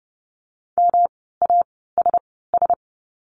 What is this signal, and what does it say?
Morse code, heard as a keyed tone, about 30 wpm, spaced as for 17 wpm, 700 Hz. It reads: MASS
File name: course / 30:10wpm / 108 Lesson 18f - Letter m - Words 30wpm